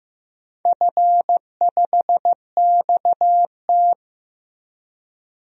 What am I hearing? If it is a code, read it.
Morse code, heard as a keyed tone, about 15 wpm, 700 Hz. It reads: F5XT
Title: DXCC-100 Call Signs - 4 Characters - Repeated 3x - 15wpm